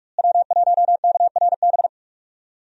Morse code, heard as a keyed tone, about 46 wpm, 700 Hz. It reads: W1KRB
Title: USA Call Signs - 5 Characters - Repeated 2x - 45wpm